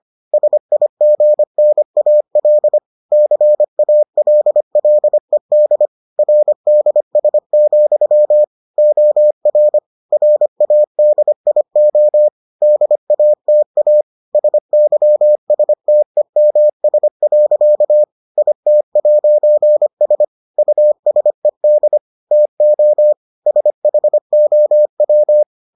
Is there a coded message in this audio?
Yes: Morse code from 0.3 to 25.4 s, heard as a keyed tone, about 25 wpm, 600 Hz. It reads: SIGNAL CALLED RDS, OR RADIO DATA SYSTEMS. IT'S USED TO SHOW